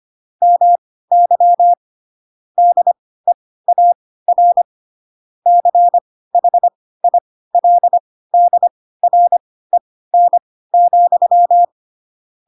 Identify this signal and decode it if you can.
Morse code, heard as a keyed tone, about 25 wpm, spaced as for 16 wpm, 700 Hz. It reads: MY DEAR CHILDREN,
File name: Pinocchio / Pinocchio ch4 10 25 ik1hge